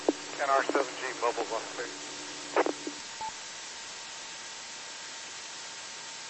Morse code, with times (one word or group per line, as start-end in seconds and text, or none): none